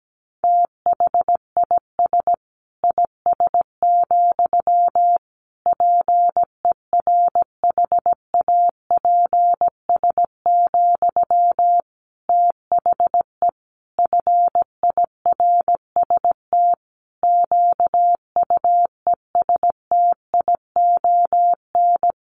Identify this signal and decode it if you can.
Morse code, heard as a keyed tone, about 17 wpm, 700 Hz. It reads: THIS IS, PERHAPS, THE FIRST QUESTION
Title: Sentences from Top 1500 Words 17wpm